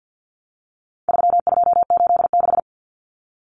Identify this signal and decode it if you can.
Morse code, heard as a keyed tone, about 50 wpm, 700 Hz. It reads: 3286
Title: Common Cut Numbers - 4-Digit 50wpm